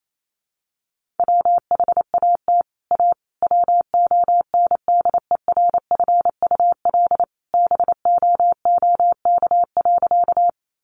Morse code, heard as a keyed tone, about 28 wpm, 700 Hz. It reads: WHAT A WONDERFUL BOOK.
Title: Sentences from Top 1100 Words 28wpm